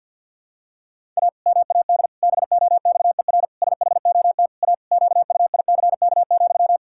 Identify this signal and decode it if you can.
Morse code, heard as a keyed tone, about 52 wpm, 700 Hz. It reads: A MAD BOXER SHOT A QUICK,
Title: Pangram Challenge v2 - 50wpm